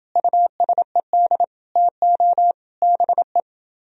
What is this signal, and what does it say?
Morse code, heard as a keyed tone, about 29 wpm, 700 Hz. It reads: USED TO BE